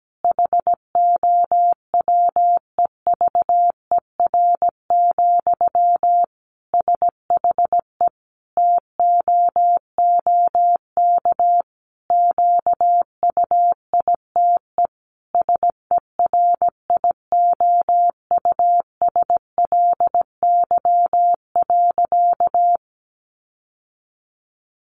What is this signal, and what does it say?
Morse code, heard as a keyed tone, about 17 wpm, 700 Hz. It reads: HOWEVER, SHE TOOK QUITE SERIOUSLY.
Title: Sentences from Top 1200 Words 17wpm